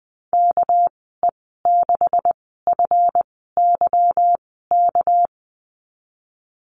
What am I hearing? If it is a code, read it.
Morse code, heard as a keyed tone, about 20 wpm, spaced as for 15 wpm, 700 Hz. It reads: KE6FYK
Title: 219 Lesson 35g - Number 0 - Call Signs 20wpm